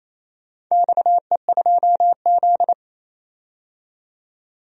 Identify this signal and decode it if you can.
Morse code, heard as a keyed tone, about 28 wpm, 700 Hz. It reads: XE2Z
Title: DXCC-100 Call Signs - 4 Characters - 28wpm